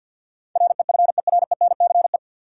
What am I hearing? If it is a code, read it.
Morse code, heard as a keyed tone, about 51 wpm, 700 Hz. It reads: REFERENCE
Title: Sentences from Top 1200 Words - New Word Drill Alt - 50wpm